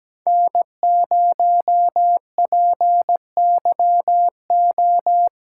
Morse code, heard as a keyed tone, about 20 wpm, 700 Hz. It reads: N0PYO
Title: Speed Racing - USA 1x3 Call Signs 17wpm